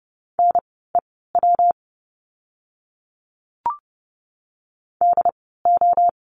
Morse code, heard as a keyed tone, about 30 wpm, spaced as for 17 wpm, 700 Hz. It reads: NEW DO
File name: course / 30:10wpm / 230 Lesson 37f - Letter x - Words 30wpm